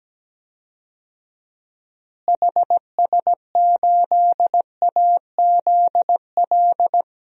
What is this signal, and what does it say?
Morse code, heard as a keyed tone, about 17 wpm, 700 Hz. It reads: HS8AZL
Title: DXCC-100 Call Signs - 6 Characters - Repeated 2x - 17wpm